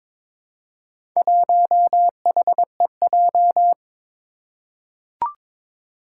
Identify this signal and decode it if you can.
Morse code, heard as a keyed tone, about 22 wpm, 700 Hz. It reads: 1HEJ
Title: Four Challenging Letter-Number Combinations 22wpm